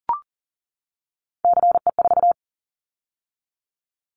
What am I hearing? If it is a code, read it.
Morse code, heard as a keyed tone, about 40 wpm, 700 Hz. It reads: CE4
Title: Callsign Prefixes for Top 100 DXCC - 40wpm